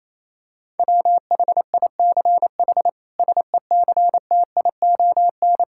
Morse code, heard as a keyed tone, about 28 wpm, 700 Hz. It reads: WHICH SECTION
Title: Sentences from Top 1000 Words - Review Part 5 - 28wpm